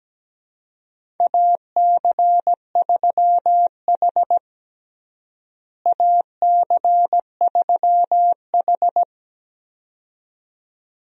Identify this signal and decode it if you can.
Morse code, heard as a keyed tone, about 17 wpm, 700 Hz. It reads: AC3H AC3H
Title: USA Call Signs - 4 Characters - Repeated 2x - 17wpm